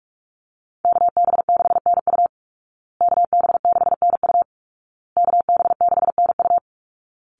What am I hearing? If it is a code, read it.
Morse code, heard as a keyed tone, about 44 wpm, 700 Hz. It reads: KB6NU KB6NU KB6NU
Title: Call Signs Repeated 2x - 45wpm